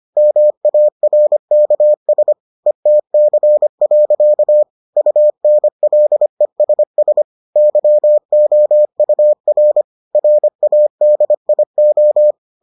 Morse code, heard as a keyed tone, about 25 wpm, 600 Hz. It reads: MARKS ETC. UNLESS YOUR RADIO